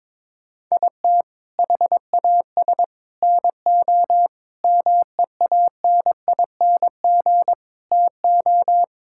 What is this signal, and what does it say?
Morse code, heard as a keyed tone, about 21 wpm, 700 Hz. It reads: IT HAS NO MEANING TO
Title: Sentences from Top 1000 Words - Review Part 6 - 22wpm